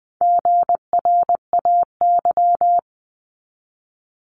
Morse code, heard as a keyed tone, about 20 wpm, 700 Hz. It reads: GRAY
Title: 260 Lesson 42e - Words 20-20wpm